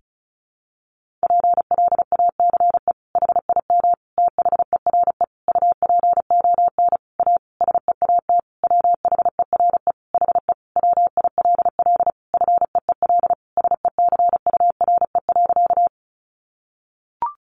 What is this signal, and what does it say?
Morse code, heard as a keyed tone, about 34 wpm, 700 Hz. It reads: PLACE HIM THERE UPON A SEAT WHERE HE WILL FEEL SECURE.
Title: Sentences from Top 1500 Words 35wpm